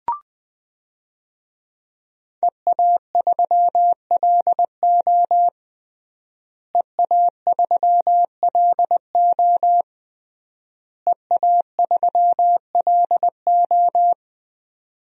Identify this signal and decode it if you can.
Morse code, heard as a keyed tone, about 20 wpm, 700 Hz. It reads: EA3LO EA3LO EA3LO
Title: DXCC-100 Call Signs - 5 Characters - Repeated 3x - 20wpm